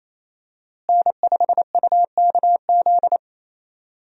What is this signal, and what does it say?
Morse code, heard as a keyed tone, about 28 wpm, 700 Hz. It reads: N5UKZ
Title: USA Call Signs - 5 Characters - Repeated 2x - 28wpm